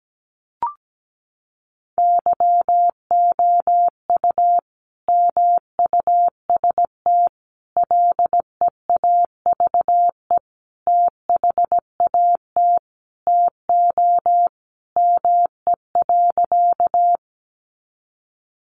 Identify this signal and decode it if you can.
Morse code, heard as a keyed tone, about 17 wpm, 700 Hz. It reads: YOU MUST LEAVE THAT TO ME.
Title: Sentences from Top 400 Words 17wpm